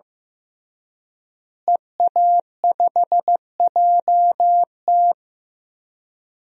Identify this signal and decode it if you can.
Morse code, heard as a keyed tone, about 14 wpm, 700 Hz. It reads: EA5JT
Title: DXCC-100 Call Signs - 5 Characters - Repeated 3x - 15wpm